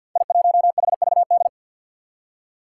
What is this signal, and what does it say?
Morse code, heard as a keyed tone, about 53 wpm, 700 Hz. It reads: I1HVD
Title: Five Challenging Letter-Number Combinations 50wpm